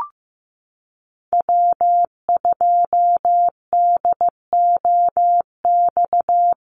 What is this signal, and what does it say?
Morse code, heard as a keyed tone, about 15 wpm, 700 Hz. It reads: W2DOX
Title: USA Call Signs 15wpm